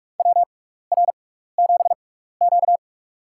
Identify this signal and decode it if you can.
Morse code, heard as a keyed tone, about 44 wpm, spaced as for 15 wpm, 700 Hz. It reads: WR7Q
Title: USA Call Signs - 6x Character Spacing - 45wpm